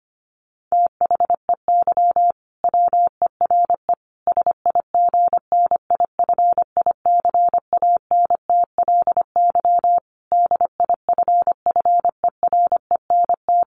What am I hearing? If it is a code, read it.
Morse code, heard as a keyed tone, about 25 wpm, 700 Hz. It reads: THEY WERE SIGNIFICANTLY DIFFERENT
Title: Sentences from Top 1500 Words 25wpm